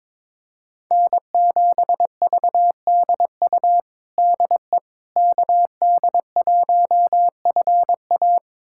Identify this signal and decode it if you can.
Morse code, heard as a keyed tone, about 22 wpm, 700 Hz. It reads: N7VDU DE KD1FA
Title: USA Callsign DE Callsign - 5 Characters - 22wpm